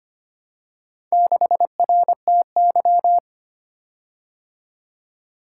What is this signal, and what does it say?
Morse code, heard as a keyed tone, about 25 wpm, 700 Hz. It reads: 6RTY
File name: practice / 25wpm / Four Letter-Number Combinations - 25wpm